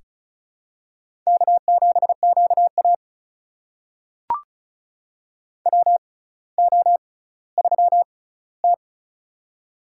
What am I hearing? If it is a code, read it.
Morse code, heard as a keyed tone, about 36 wpm, spaced as for 18 wpm, 700 Hz. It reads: K7QA WO3T
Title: USA Call Signs - 4 Characters - 6x Character Spacing - 35wpm